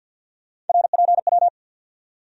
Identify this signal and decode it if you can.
Morse code, heard as a keyed tone, about 46 wpm, 700 Hz. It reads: RPW
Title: USA Call Sign Postfixes - 3 Letters 50wpm